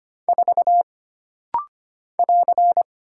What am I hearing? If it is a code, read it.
Morse code, heard as a keyed tone, about 25 wpm, 700 Hz. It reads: 4 <AR>
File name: Single Character (Letter:Number:Punctuation:Prosign) - Rapid-Fire - 25wpm